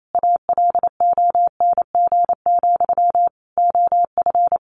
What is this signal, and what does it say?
Morse code, heard as a keyed tone, about 28 wpm, 700 Hz. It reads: ALONG, OF
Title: Sentences from Top 700 Words 28wpm